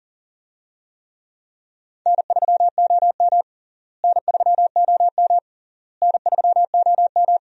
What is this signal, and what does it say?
Morse code, heard as a keyed tone, about 40 wpm, 700 Hz. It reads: N3OM N3OM N3OM
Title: USA Call Signs - 4 Characters - Repeated 3x - 40wpm